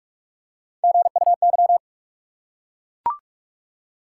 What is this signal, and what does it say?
Morse code, heard as a keyed tone, about 44 wpm, 700 Hz. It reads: GUY
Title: Three Letter Words - 45wpm